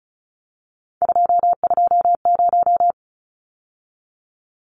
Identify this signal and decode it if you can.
Morse code, heard as a keyed tone, about 34 wpm, 700 Hz. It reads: 220
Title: Numbers - 3-Digit 35wpm